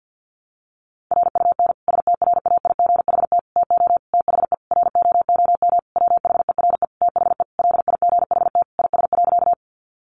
Morse code, heard as a keyed tone, about 50 wpm, 700 Hz. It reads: RUN STRAIGHT TO THE ROOM WHERE THE LIGHT IS.
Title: Sentences from Top 1000 Words 50wpm